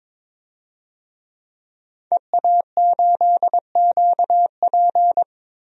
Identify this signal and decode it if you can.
Morse code, heard as a keyed tone, about 22 wpm, 700 Hz. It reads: EA8QP
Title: DXCC-100 Call Signs - 5 Characters - Repeated 3x - 22wpm